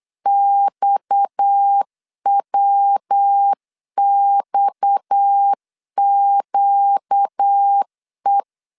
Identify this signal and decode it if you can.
Morse code, heard as a keyed tone, about 8 wpm, 800 Hz. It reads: XWXQE
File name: LEZIONI / 6 PKX0 / 6 PKX 8wpm 3041